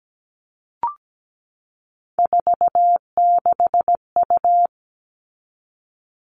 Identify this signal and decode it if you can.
Morse code, heard as a keyed tone, about 17 wpm, 700 Hz. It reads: 46U
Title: All Cut Numbers - 3-Digit 17wpm